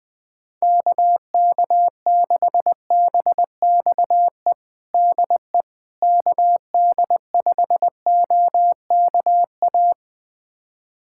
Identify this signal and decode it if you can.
Morse code, heard as a keyed tone, about 20 wpm, 700 Hz. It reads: KK6BXE DE KD5OKA
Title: USA Callsign DE Callsign - 6 Characters - 20wpm